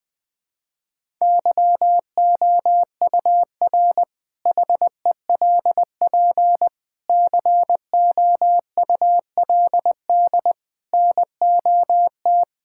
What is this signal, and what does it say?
Morse code, heard as a keyed tone, about 20 wpm, 700 Hz. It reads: YOUR HELP COULD NOT
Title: Sentences with 8-Word Phrases - 20wpm